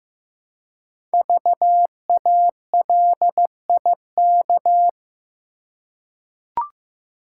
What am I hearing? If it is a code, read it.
Morse code, heard as a keyed tone, about 15 wpm, 700 Hz. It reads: VALIK
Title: Code Groups - 5 Letters - 15wpm